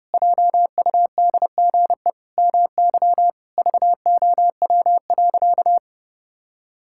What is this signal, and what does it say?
Morse code, heard as a keyed tone, about 32 wpm, 700 Hz. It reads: JUDGE MY VOW.
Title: Pangram Challenge v2 - 30wpm